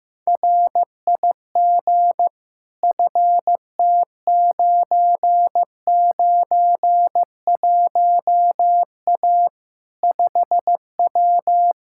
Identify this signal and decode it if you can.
Morse code, heard as a keyed tone, about 15 wpm, 700 Hz. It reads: RIG FT991A 5W
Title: Practice QSO - Exchange 2 - Info - 15wpm